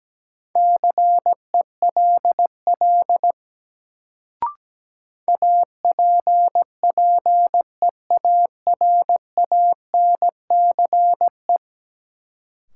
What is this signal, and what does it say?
Morse code, heard as a keyed tone, about 17 wpm, 700 Hz. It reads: CELL APPEARANCE
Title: Sentences from Top 1100 Words - New Word Drill Alt - 17wpm